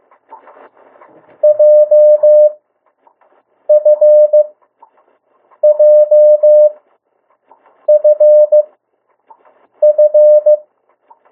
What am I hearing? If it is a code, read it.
Morse code, heard as a keyed tone, about 15 wpm, spaced as for 6 wpm, 600 Hz. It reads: JFJFF
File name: urok 1